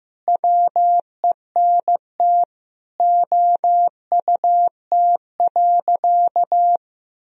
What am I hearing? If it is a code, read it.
Morse code, heard as a keyed tone, about 15 wpm, 700 Hz. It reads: WENT OUT.
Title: Sentences from Top 1000 Words - Review Part 1 - 15wpm